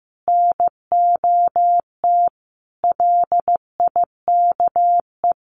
Morse code, heard as a keyed tone, about 15 wpm, 700 Hz. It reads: NOT LIKE
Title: Sets of 5 Words 15wpm